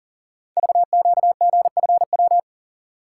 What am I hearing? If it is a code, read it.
Morse code, heard as a keyed tone, about 40 wpm, 700 Hz. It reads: VQGFW
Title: Code Groups - 5 Letters - 40wpm